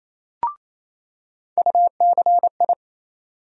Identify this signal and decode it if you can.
Morse code, heard as a keyed tone, about 28 wpm, 700 Hz. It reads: UCI
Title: USA Call Sign Postfixes - 3 Letters 28wpm